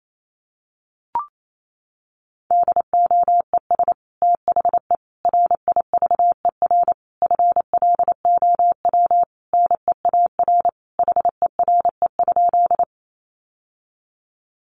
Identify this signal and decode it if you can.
Morse code, heard as a keyed tone, about 28 wpm, 700 Hz. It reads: DOES THE RIVER FLOW NEAR HERE?
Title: Sentences from Top 1200 Words 28wpm